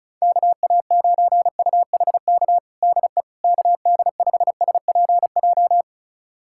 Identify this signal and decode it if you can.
Morse code, heard as a keyed tone, about 34 wpm, 700 Hz. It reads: KA9UHK DE KD5SPJ
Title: USA Callsign DE Callsign - 6 Characters - 35wpm